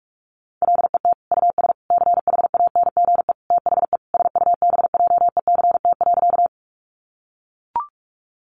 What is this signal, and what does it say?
Morse code, heard as a keyed tone, about 44 wpm, 700 Hz. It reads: LET US CHANGE THE SUBJECT.
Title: Sentences from Top 800 Words 45wpm